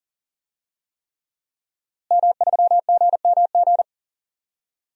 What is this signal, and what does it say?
Morse code, heard as a keyed tone, about 38 wpm, 700 Hz. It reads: M3GMG
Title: DXCC-100 Call Signs - 5 Characters - Repeated 3x - 40wpm